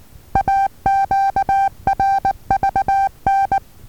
Morse code, heard as a keyed tone, about 19 wpm, 780 Hz. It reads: AQRVN
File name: Taviro HA6VA B